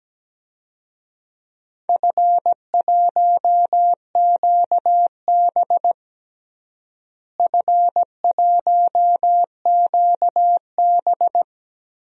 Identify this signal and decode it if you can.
Morse code, heard as a keyed tone, about 17 wpm, 700 Hz. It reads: F1QB F1QB
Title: DXCC-100 Call Signs - 4 Characters - Repeated 2x - 17wpm